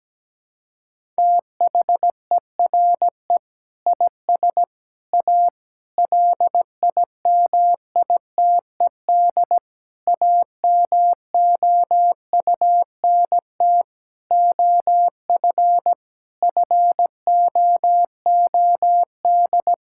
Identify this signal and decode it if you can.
Morse code, heard as a keyed tone, about 17 wpm, 700 Hz. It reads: THERE IS A LIMITED AMOUNT OF FOOD
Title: Sentences from Top 1000 Words - Review Part 6 - 17wpm